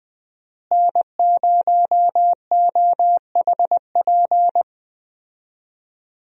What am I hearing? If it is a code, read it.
Morse code, heard as a keyed tone, about 20 wpm, 700 Hz. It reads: N0OHP
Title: USA Call Signs - 5 Characters - Repeated 3x - 20wpm